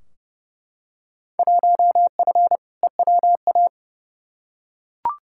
Morse code, heard as a keyed tone, about 32 wpm, 700 Hz. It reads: 1F EWA
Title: Field Day Exchanges Encore - 30wpm